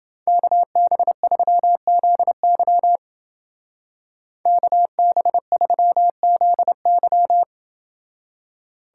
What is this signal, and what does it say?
Morse code, heard as a keyed tone, about 28 wpm, 700 Hz. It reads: KB3ZY KB3ZY